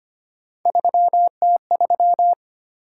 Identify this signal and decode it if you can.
Morse code, heard as a keyed tone, about 25 wpm, 700 Hz. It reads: 3T3
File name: Three Letter-Number Combinations - 25wpm